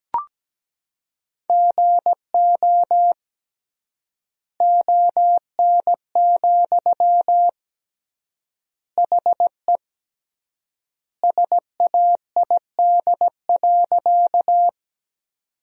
Morse code, heard as a keyed tone, about 17 wpm, 700 Hz. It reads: GO ON, HE SAID.
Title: 3x Word-Spacing - Sentences from Top 100 Words - 17wpm